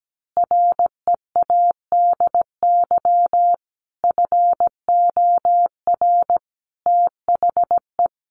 Morse code, heard as a keyed tone, about 17 wpm, 700 Hz. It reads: READY FOR THE
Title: Sentences from Top 1500 Words 17wpm